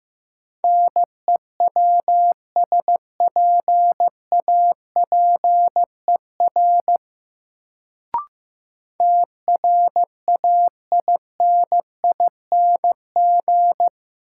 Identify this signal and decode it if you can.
Morse code, heard as a keyed tone, about 15 wpm, 700 Hz. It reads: NEWSPAPER TRAINING
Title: Sentences from Top 1000 Words - Review Part 4 - New Word Drill - 15wpm